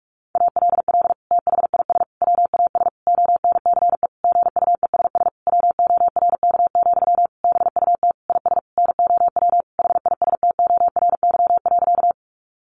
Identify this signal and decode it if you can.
Morse code, heard as a keyed tone, about 44 wpm, 700 Hz. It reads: ALL THIS WAS ONCE GUESS WORK, BUT IS NOW HISTORY.
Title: Sentences from Top 1500 Words 45wpm